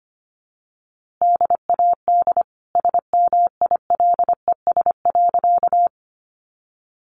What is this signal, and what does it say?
Morse code, heard as a keyed tone, about 25 wpm, 700 Hz. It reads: DAD SMILES.
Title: Easy Phrases 25wpm